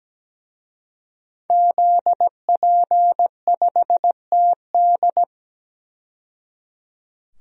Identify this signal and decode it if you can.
Morse code, heard as a keyed tone, about 17 wpm, 700 Hz. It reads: ZP5TD
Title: DXCC-100 Call Signs - 5 Characters - 17wpm